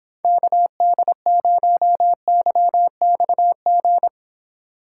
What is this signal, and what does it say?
Morse code, heard as a keyed tone, about 26 wpm, 700 Hz. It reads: KD0YXG